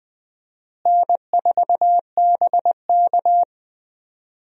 Morse code, heard as a keyed tone, about 20 wpm, 700 Hz. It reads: N4BK
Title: USA Call Signs - 4 Characters - Repeated 2x - 20wpm